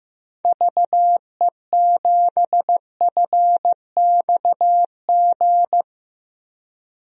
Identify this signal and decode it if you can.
Morse code, heard as a keyed tone, about 15 wpm, 700 Hz. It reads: VE7FXG
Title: DXCC-100 Call Signs - 6 Characters - Repeated 2x - 15wpm